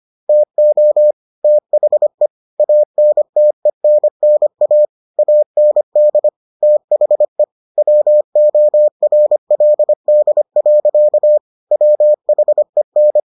Morse code, heard as a keyed tone, about 25 wpm, 600 Hz. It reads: TO THE ANTENNA AND THE WORLD. WHEN